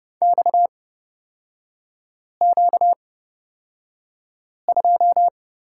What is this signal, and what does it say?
Morse code, heard as a keyed tone, about 30 wpm, spaced as for 5 wpm, 700 Hz. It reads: XQ2
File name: Lesson 38e - Letter q - Three Characters 30wpm